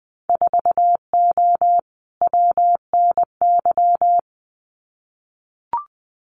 Morse code, heard as a keyed tone, about 20 wpm, 700 Hz. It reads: 4O WNY
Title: Winter Field Day Exchanges Encore 20wpm